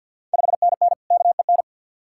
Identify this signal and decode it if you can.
Morse code, heard as a keyed tone, about 46 wpm, 700 Hz. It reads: 5NN KEN
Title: Ontario QSO Party Exchange 50wpm